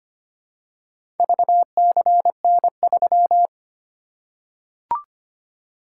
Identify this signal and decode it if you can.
Morse code, heard as a keyed tone, about 25 wpm, 700 Hz. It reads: VCN3
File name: practice / 25wpm / Four Letter-Number Combinations - 25wpm